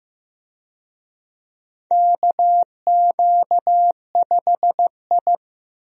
Morse code, heard as a keyed tone, about 15 wpm, 700 Hz. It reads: KQ5I
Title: USA Call Signs - 4 Characters - 15wpm